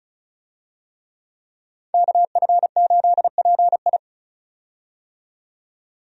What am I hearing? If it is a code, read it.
Morse code, heard as a keyed tone, about 36 wpm, 700 Hz. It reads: KF8PI
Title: USA Call Signs - 5 Characters - 35wpm